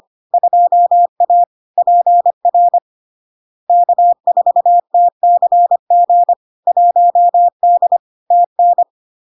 Morse code, heard as a keyed tone, about 25 wpm, 700 Hz. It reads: 2A PR K4TCG 1D TN
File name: LARK-FD-2022-Log-in-CW-25wpm-700-Hz